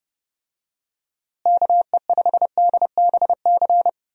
Morse code, heard as a keyed tone, about 29 wpm, 700 Hz. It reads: KE5DBC
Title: USA Call Signs - 6 Characters - Repeated 2x - 30wpm